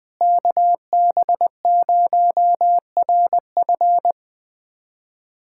Morse code, heard as a keyed tone, about 20 wpm, 700 Hz. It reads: KB0RF
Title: USA Callsign DE Callsign - 5 Characters - 20wpm